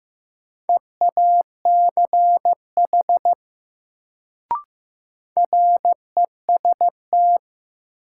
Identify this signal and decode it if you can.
Morse code, heard as a keyed tone, about 15 wpm, 700 Hz. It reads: EACH REST
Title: Sets of 2 Words - New Word Drill - 15wpm